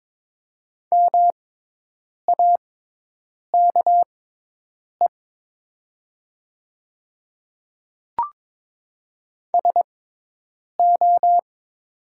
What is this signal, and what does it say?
Morse code, heard as a keyed tone, about 22 wpm, spaced as for 8 wpm, 690 Hz. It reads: MAKE SO
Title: 6x Character Spacing - Top 100 Words - 22wpm